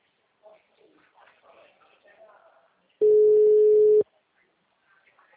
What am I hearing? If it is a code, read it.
Ringback tone (Europe, 425 Hz), with speech.